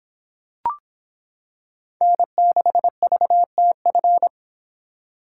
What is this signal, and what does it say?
Morse code, heard as a keyed tone, about 26 wpm, 700 Hz. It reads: N6VTF